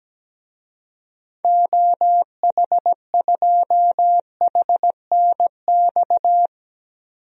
Morse code, heard as a keyed tone, about 17 wpm, 700 Hz. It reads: OH2HNX
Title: DXCC-100 Call Signs - 6 Characters - 17wpm